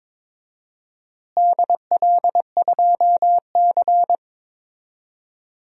Morse code, heard as a keyed tone, about 21 wpm, 700 Hz. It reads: DL2C
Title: DXCC-100 Call Signs - 4 Characters - Repeated 2x - 22wpm